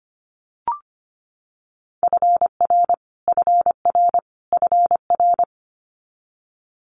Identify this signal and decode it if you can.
Morse code, heard as a keyed tone, about 24 wpm, 700 Hz. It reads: FR FR FR
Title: Multi-letter Phonemes Repeated 3x 25wpm